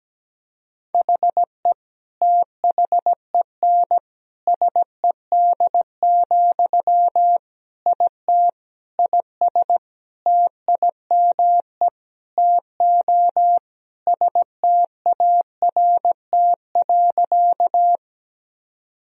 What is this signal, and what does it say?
Morse code, heard as a keyed tone, about 17 wpm, 700 Hz. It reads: HE THEN SED, IT IS TIME TO START.
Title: Sentences from Top 300 Words and Abbreviations 17wpm